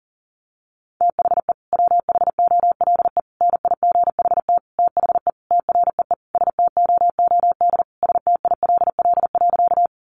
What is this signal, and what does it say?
Morse code, heard as a keyed tone, about 40 wpm, 700 Hz. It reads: THE WHOLE NIGHT THE TREE STOOD STILL.
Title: Sentences from Top 500 Words 40wpm